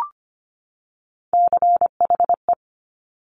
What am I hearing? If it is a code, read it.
Morse code, heard as a keyed tone, about 25 wpm, 700 Hz. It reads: CHE
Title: Top Three Letter Combinations in Words 25wpm